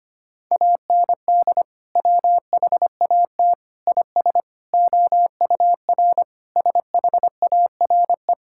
Morse code, heard as a keyed tone, about 25 wpm, 700 Hz. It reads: AND WHAT IS OUR SHARE
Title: Sentences from Top 1000 Words - Review Part 6 - 25wpm